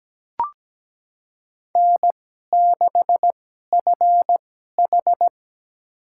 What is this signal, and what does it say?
Morse code, heard as a keyed tone, about 17 wpm, spaced as for 12 wpm, 700 Hz. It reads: N6FH
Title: USA Call Signs - 2x Character Spacing - 17wpm